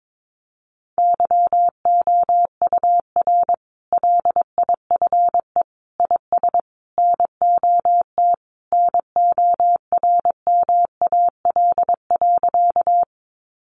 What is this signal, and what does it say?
Morse code, heard as a keyed tone, about 22 wpm, 700 Hz. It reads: YOUR LIFE IS NOT NORMAL.
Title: Sentences from Top 1200 Words 22wpm